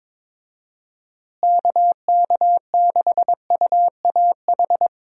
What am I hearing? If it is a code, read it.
Morse code, heard as a keyed tone, about 22 wpm, 700 Hz. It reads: KK6UAH
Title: USA Call Signs - 6 Characters - Repeated 3x - 22wpm